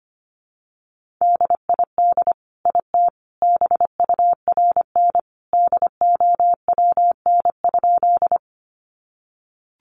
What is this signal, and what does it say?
Morse code, heard as a keyed tone, about 25 wpm, 700 Hz. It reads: DID IT BURN DOWN?
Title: Sentences from Top 1300 Words 25wpm